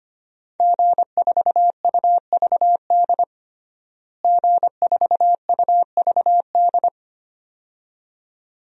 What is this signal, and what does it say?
Morse code, heard as a keyed tone, about 25 wpm, 700 Hz. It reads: G4UVD G4UVD
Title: DXCC-100 Call Signs - 5 Characters - Repeated 2x - 25wpm